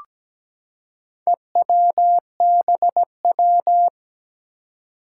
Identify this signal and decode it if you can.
Morse code, heard as a keyed tone, about 17 wpm, 700 Hz. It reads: EWBW
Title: Four Challenging Letter-Number Combinations 17wpm